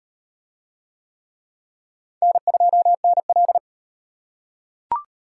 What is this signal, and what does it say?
Morse code, heard as a keyed tone, about 37 wpm, 700 Hz. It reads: N2NL